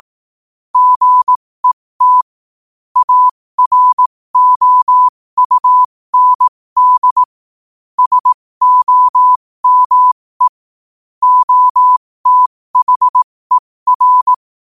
Morse code, heard as a keyed tone, about 18 wpm, spaced as for 15 wpm, 1000 Hz. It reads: GET AROUND SOME OTHER
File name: qotdfiles / 15wpm